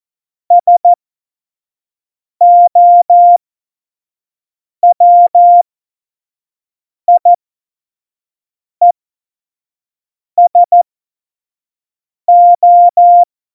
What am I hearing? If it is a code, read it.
Morse code, heard as a keyed tone, about 14 wpm, spaced as for 5 wpm, 700 Hz. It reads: SOWIESO
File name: Morsekurs / Witze / Witz 27. November 2018, 25 F70